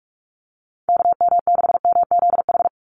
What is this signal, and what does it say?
Morse code, heard as a keyed tone, about 44 wpm, 700 Hz. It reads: KM6MZH
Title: USA Call Signs 45wpm